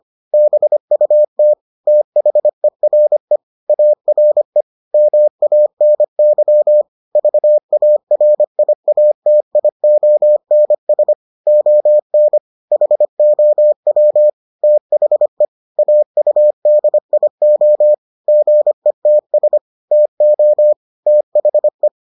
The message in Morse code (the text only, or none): BUT THERE ARE MANY VARIATIONS ON HOW THE AUDIO GETS TO THE